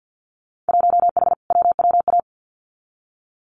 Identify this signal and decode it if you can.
Morse code, heard as a keyed tone, about 50 wpm, 700 Hz. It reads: 1H WWA
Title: Winter Field Day Exchanges 50wpm